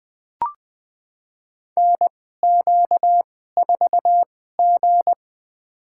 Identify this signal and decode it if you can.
Morse code, heard as a keyed tone, about 20 wpm, spaced as for 14 wpm, 700 Hz. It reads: NQ4G
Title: USA Call Signs - 4 Characters - 2x Character Spacing - 20wpm